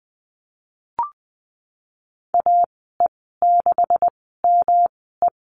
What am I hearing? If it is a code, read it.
Morse code, heard as a keyed tone, about 20 wpm, spaced as for 15 wpm, 700 Hz. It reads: AE6ME
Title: 193 Lesson 31g - Letter v - Call Signs 20wpm